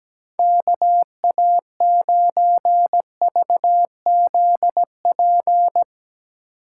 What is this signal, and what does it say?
Morse code, heard as a keyed tone, about 17 wpm, 700 Hz. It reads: KA9VZP
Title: USA Call Signs - 6 Characters - 8x Character Spacing - 17wpm